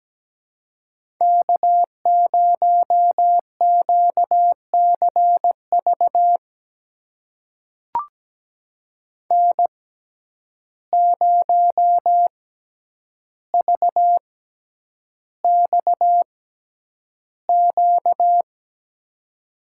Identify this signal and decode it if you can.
Morse code, heard as a keyed tone, about 17 wpm, spaced as for 9 wpm, 700 Hz. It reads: K0QCV N0VXQ
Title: USA Call Signs - 5 Characters - 6x Character Spacing - 17wpm